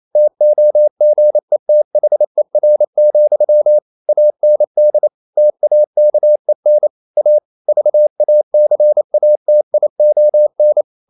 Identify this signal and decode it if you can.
Morse code, heard as a keyed tone, about 28 wpm, 600 Hz. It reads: TOGETHER, AND TAKEN A VACATION
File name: Never 0027wpm